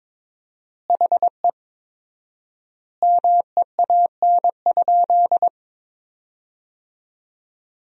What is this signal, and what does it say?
Morse code, heard as a keyed tone, about 22 wpm, 700 Hz. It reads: HE MEAN?
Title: 4x Word-Spacing - Sentences from Top 200 Words - 22wpm